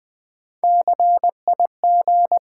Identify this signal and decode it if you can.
Morse code, heard as a keyed tone, about 20 wpm, 700 Hz. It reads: CIG
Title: Build A Word - 3 Letters At A Time - 9-Letter Words - 20wpm